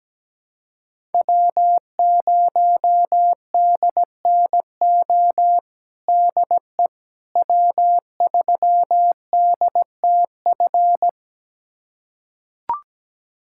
Morse code, heard as a keyed tone, about 17 wpm, 700 Hz. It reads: W0DNO DE W3DTF
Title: USA Callsign DE Callsign - 5 Characters - 17wpm